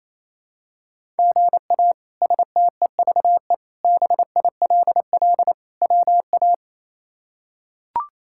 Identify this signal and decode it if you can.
Morse code, heard as a keyed tone, about 28 wpm, 700 Hz. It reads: GA STEVE BILL WA
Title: K1USN SST QSO Practice - 28wpm